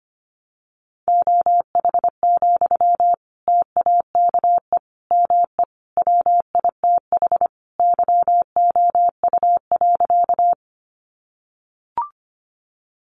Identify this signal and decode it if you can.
Morse code, heard as a keyed tone, about 25 wpm, 700 Hz. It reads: OH, TAKE ME WITH YOU.
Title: Sentences from Top 500 Words 25wpm